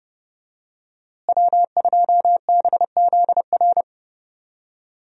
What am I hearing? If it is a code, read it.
Morse code, heard as a keyed tone, about 30 wpm, 700 Hz. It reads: W2BZR
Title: USA Call Signs - 5 Characters - Repeated 3x - 30wpm